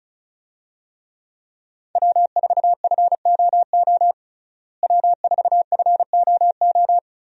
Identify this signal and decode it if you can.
Morse code, heard as a keyed tone, about 36 wpm, 700 Hz. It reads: W4FOO W4FOO
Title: USA Call Signs - 5 Characters - Repeated 3x - 35wpm